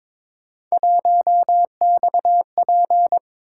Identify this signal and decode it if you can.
Morse code, heard as a keyed tone, about 22 wpm, 700 Hz. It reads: 1XP
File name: practice / 22wpm / Three Letter-Number Combinations - 22wpm